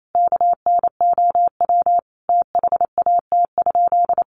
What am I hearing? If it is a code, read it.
Morse code, heard as a keyed tone, about 28 wpm, 700 Hz. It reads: KNOW THAT?
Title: Sentences from Top 600 Words 28wpm